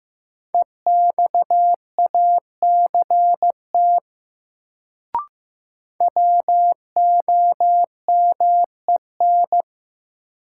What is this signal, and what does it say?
Morse code, heard as a keyed tone, about 15 wpm, 700 Hz. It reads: EXACT WOMEN
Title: Sentences from Top 1300 Words - New Word Drill Alt - 15wpm